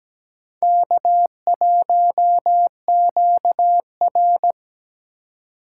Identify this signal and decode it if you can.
Morse code, heard as a keyed tone, about 20 wpm, 700 Hz. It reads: K1QR
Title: Speed Racing - USA 1x2 Call Signs 17wpm